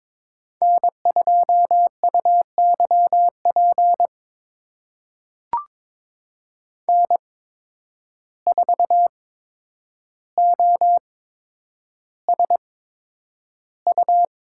Morse code, heard as a keyed tone, about 22 wpm, spaced as for 9 wpm, 700 Hz. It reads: N2UYP N4OSU
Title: USA Call Signs - 5 Characters - 8x Character Spacing - 22wpm